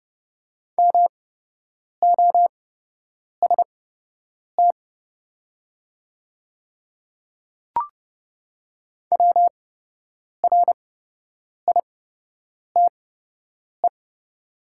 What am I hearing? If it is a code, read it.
Morse code, heard as a keyed tone, about 29 wpm, spaced as for 8 wpm, 700 Hz. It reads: MOST WRITE
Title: 8x Character Spacing - Top 100 Words - 30wpm